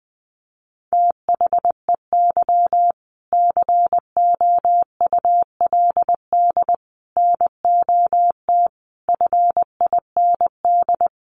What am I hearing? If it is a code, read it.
Morse code, heard as a keyed tone, about 20 wpm, 700 Hz. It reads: THEY COULD NOT FIND
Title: Sentences from Top 1400 Words 20wpm